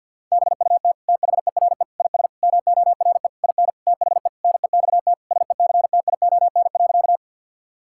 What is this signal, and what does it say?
Morse code, heard as a keyed tone, about 50 wpm, 700 Hz. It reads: BUT THERE IS MORE IN THE NEXT SECTION.
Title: Sentences from Top 1000 Words - Review Part 5 - 50wpm